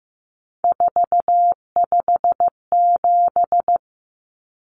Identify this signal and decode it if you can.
Morse code, heard as a keyed tone, about 15 wpm, 700 Hz. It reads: 457
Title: Numbers - 3-Digit 15wpm